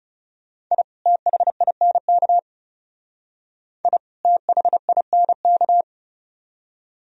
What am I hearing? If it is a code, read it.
Morse code, heard as a keyed tone, about 36 wpm, 700 Hz. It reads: I THINK I THINK
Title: Speed-Racing - Sets of 2 Words 30wpm